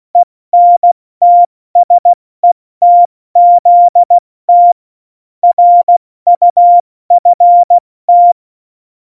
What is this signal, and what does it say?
Morse code, heard as a keyed tone, about 16 wpm, 700 Hz. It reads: ENTSETZT RUFT
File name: Morsekurs / Witze / Witz 12. Dezember 2017, 70 F80